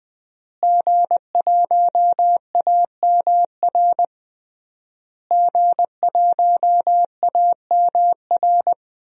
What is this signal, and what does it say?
Morse code, heard as a keyed tone, about 19 wpm, 700 Hz. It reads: G1AMR G1AMR
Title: DXCC-100 Call Signs - 5 Characters - Repeated 3x - 20wpm